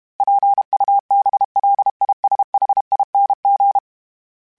Morse code, heard as a keyed tone, about 32 wpm, 800 Hz. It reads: PUBLISHING
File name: Special INGs-mp3s / Special INGs-mp3s / Sngl Ing @30